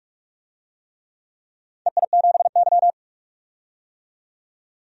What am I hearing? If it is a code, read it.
Morse code, heard as a keyed tone, about 42 wpm, 700 Hz. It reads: EI7Y
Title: DXCC-100 Call Signs - 4 Characters - 45wpm